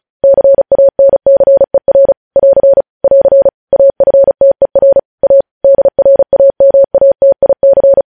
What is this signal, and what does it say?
Morse code, heard as a keyed tone, about 36 wpm, 550 Hz. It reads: CANCER <AR> <AR> AFTER A DRAMATIC
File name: podcast / News headlines-1x-35x35